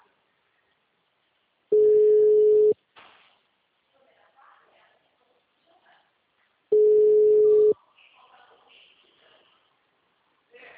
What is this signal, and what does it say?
Ringback tone (Europe, 425 Hz), with speech.